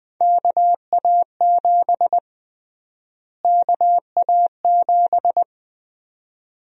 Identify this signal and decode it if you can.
Morse code, heard as a keyed tone, about 20 wpm, 700 Hz. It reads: KA7 KA7
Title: USA Call Sign Prefixes - Repeated 2x - 20wpm